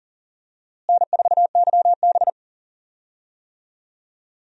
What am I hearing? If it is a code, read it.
Morse code, heard as a keyed tone, about 40 wpm, 700 Hz. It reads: N4YB
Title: USA Call Signs - 4 Characters - Repeated 3x - 40wpm